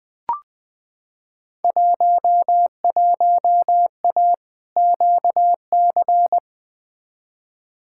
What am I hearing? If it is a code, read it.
Morse code, heard as a keyed tone, about 21 wpm, 700 Hz. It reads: 11A QC
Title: Field Day Exchanges Encore - 20wpm